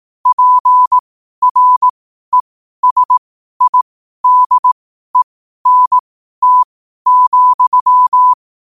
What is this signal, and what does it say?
Morse code, heard as a keyed tone, about 18 wpm, spaced as for 12 wpm, 1000 Hz. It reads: PRESIDENT,
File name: qotdfiles / 12wpm